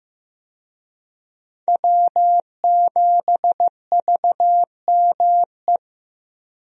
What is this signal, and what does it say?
Morse code, heard as a keyed tone, about 15 wpm, 700 Hz. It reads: W7VME
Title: USA Call Signs - 5 Characters - Repeated 3x - 15wpm